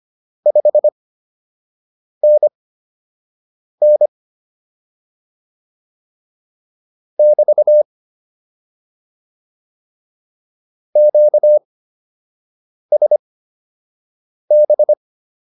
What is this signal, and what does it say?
Morse code, heard as a keyed tone, about 25 wpm, spaced as for 6 wpm, 600 Hz. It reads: 5NN <BT> QSB